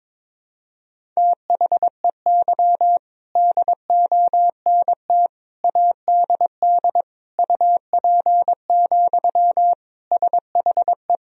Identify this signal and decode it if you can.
Morse code, heard as a keyed tone, about 22 wpm, 700 Hz. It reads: THEY DONT ADD UP, SHE
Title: Sentences from Top 1000 Words - Review Part 1 - 22wpm